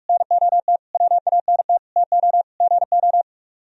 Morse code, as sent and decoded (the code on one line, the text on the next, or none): -. --- - / .-- .- -. - / - --- / --. ---
NOT WANT TO GO